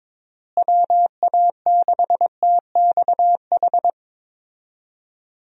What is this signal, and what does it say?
Morse code, heard as a keyed tone, about 22 wpm, 700 Hz. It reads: WA6TXH
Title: USA Call Signs - 6 Characters - Repeated 2x - 22wpm